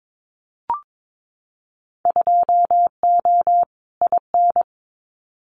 Morse code, heard as a keyed tone, about 22 wpm, 700 Hz. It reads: 2O IN
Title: Winter Field Day Exchanges 22wpm